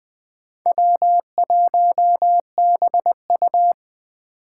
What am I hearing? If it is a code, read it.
Morse code, heard as a keyed tone, about 20 wpm, 700 Hz. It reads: W1BU